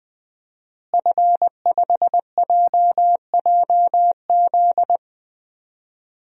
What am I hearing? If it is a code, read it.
Morse code, heard as a keyed tone, about 20 wpm, 700 Hz. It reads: F5JJZ
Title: DXCC-100 Call Signs - 5 Characters - Repeated 3x - 20wpm